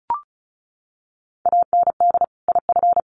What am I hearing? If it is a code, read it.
Morse code, heard as a keyed tone, about 34 wpm, 700 Hz. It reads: AND IF
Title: Sets of 2 Words 35wpm